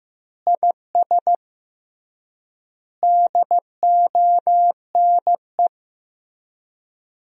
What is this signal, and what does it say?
Morse code, heard as a keyed tone, about 15 wpm, 700 Hz. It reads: IS DONE
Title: 3x Word-Spacing - Sentences from Top 500 Words - 15wpm